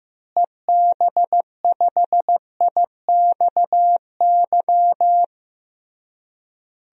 Morse code, heard as a keyed tone, about 15 wpm, 700 Hz. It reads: EB5IXY
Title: DXCC-100 Call Signs - 6 Characters - Repeated 3x - 15wpm